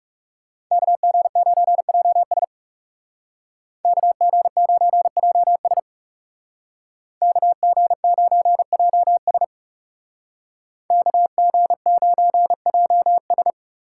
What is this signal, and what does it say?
Morse code, heard as a keyed tone, about 37 wpm, 700 Hz. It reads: KG9JS KG9JS KG9JS KG9JS
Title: Speed Racing - USA Call Signs 30wpm